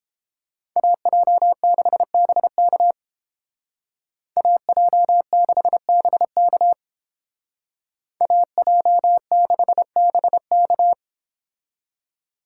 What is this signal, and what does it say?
Morse code, heard as a keyed tone, about 29 wpm, 700 Hz. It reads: AJ6BK AJ6BK AJ6BK